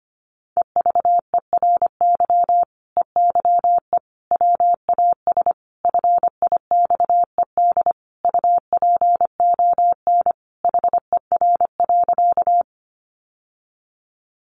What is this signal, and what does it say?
Morse code, heard as a keyed tone, about 25 wpm, 700 Hz. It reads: EVERY EYE WAS FIXED UPON HER.
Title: Sentences from Top 1100 Words 25wpm